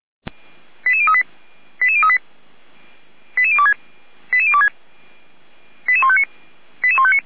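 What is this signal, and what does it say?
Five-tone selective call (ZVEI), heard as quick runs of five tones, tones of about 70 ms. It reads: 88038 88038 88026 88026 88159 88159
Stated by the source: ZVEI five-tone selective call received with an SDR.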